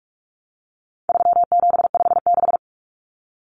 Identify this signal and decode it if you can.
Morse code, heard as a keyed tone, about 44 wpm, 700 Hz. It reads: 3756
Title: Common Cut Numbers - 4-Digit 45wpm